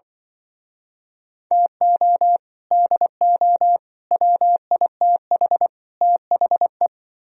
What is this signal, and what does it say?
Morse code, heard as a keyed tone, about 25 wpm, 700 Hz. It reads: TO DO WITH THE